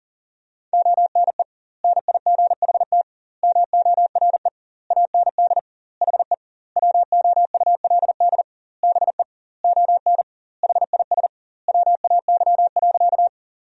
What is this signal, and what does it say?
Morse code, heard as a keyed tone, about 40 wpm, 700 Hz. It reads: ONE NIGHT MORE AND HE WOULD BE ON HIS WAY.
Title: 2x Word-Spacing - Sentences from Top 300 Words - 40wpm